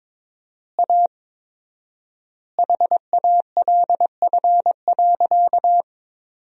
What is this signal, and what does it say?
Morse code, heard as a keyed tone, about 22 wpm, 700 Hz. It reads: A HALF.
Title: 4x Word-Spacing - Sentences from Top 400 Words - 22wpm